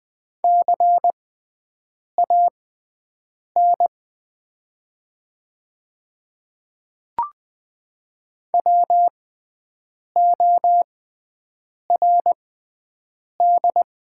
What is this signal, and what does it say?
Morse code, heard as a keyed tone, about 20 wpm, spaced as for 7 wpm, 700 Hz. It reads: CAN WORD
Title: 6x Character Spacing - Top 100 Words - 20wpm